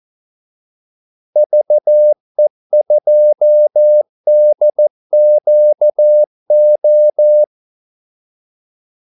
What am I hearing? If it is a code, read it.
Morse code, heard as a keyed tone, about 14 wpm, 600 Hz. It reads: VE2DQO
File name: Callsigns-14wpm